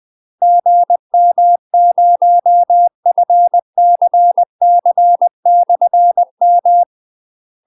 Morse code, heard as a keyed tone, about 20 wpm, 700 Hz. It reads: GM0FCC/M
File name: UK MASTER QSOs TRAINING / 20WPM / UK MASTER QSOs 0073